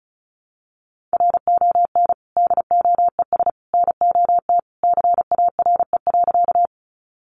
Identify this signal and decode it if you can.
Morse code, heard as a keyed tone, about 34 wpm, 700 Hz. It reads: RON DOES NOT CARE.
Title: Additional Easy Phrases 35wpm